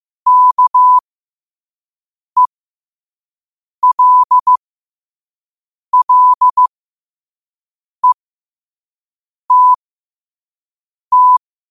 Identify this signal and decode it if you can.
Morse code, heard as a keyed tone, about 15 wpm, spaced as for 5 wpm, 1000 Hz. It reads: KELLETT
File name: qotdfiles / 5wpm